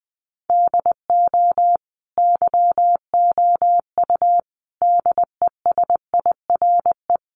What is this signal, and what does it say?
Morse code, heard as a keyed tone, about 20 wpm, 700 Hz. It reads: DO YOU DESIRE
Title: Sentences from Top 1100 Words 20wpm